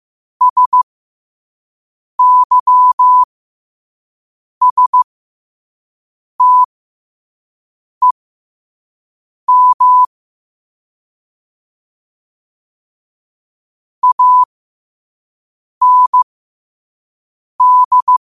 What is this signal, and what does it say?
Morse code, heard as a keyed tone, about 15 wpm, spaced as for 5 wpm, 1000 Hz. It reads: SYSTEM AND
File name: qotdfiles / 5wpm